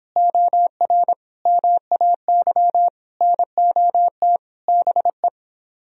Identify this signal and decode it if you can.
Morse code, heard as a keyed tone, about 31 wpm, 700 Hz. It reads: OR MAY NOT BE